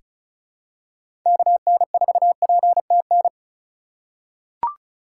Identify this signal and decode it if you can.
Morse code, heard as a keyed tone, about 36 wpm, 690 Hz. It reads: KN4PTN
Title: USA Call Signs - 6 Characters - 6x Character Spacing - 35wpm